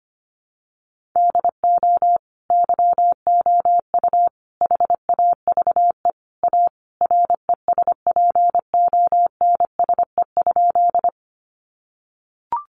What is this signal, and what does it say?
Morse code, heard as a keyed tone, about 25 wpm, 700 Hz. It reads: DO YOU HAVE A RESPONSE?
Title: Sentences from Top 1400 Words 25wpm